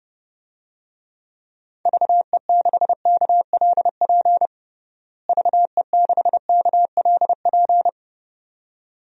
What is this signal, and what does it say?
Morse code, heard as a keyed tone, about 30 wpm, 700 Hz. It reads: VE6KLP VE6KLP
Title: DXCC-100 Call Signs - 6 Characters - Repeated 2x - 30wpm